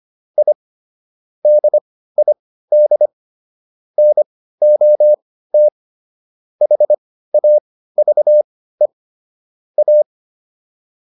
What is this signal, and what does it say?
Morse code, heard as a keyed tone, about 25 wpm, spaced as for 15 wpm, 600 Hz. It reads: I DID NOT HAVE A